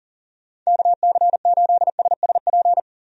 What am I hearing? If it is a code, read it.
Morse code, heard as a keyed tone, about 40 wpm, 700 Hz. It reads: KC8SSP
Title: USA Call Signs - 6 Characters - 40wpm